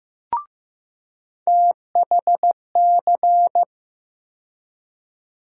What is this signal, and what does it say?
Morse code, heard as a keyed tone, about 16 wpm, 700 Hz. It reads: THC
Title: USA Call Sign Postfixes - 3 Letters 15wpm